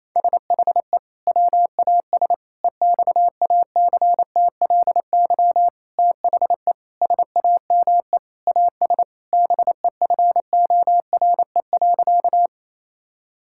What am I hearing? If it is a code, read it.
Morse code, heard as a keyed tone, about 28 wpm, 700 Hz. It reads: SHE WAS EXACTLY THE SAME AS BEFORE.
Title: Sentences from Top 1000 Words - Review Part 6 - 28wpm